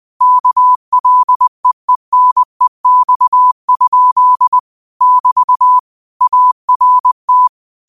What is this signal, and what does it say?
Morse code, heard as a keyed tone, about 20 wpm, 1000 Hz. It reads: KLEENEX? <BT> ART
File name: qotdfiles / 20wpm